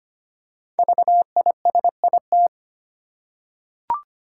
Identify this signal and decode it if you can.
Morse code, heard as a keyed tone, about 25 wpm, 700 Hz. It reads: VISIT
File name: 5 Letter Words - First 3 Letters Given - 25wpm